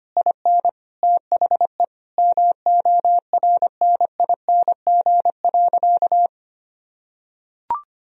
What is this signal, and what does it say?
Morse code, heard as a keyed tone, about 25 wpm, 700 Hz. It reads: IN THE MORNING.
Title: Sentences from Top 1000 Words - Review Part 2 - 25wpm